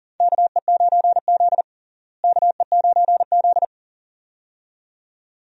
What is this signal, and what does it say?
Morse code, heard as a keyed tone, about 37 wpm, 700 Hz. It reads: KE9Z KE9Z
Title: USA Call Signs - 4 Characters - Repeated 2x - 40wpm